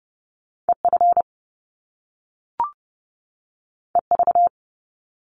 Morse code, heard as a keyed tone, about 30 wpm, 700 Hz. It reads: EF EV
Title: Top Two Letter Combinations in Words 30wpm